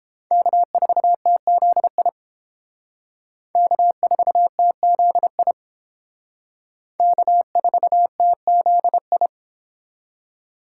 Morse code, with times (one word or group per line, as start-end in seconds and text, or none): 0.3-2.1 K4TZI
3.6-5.5 K4TZI
7.0-9.3 K4TZI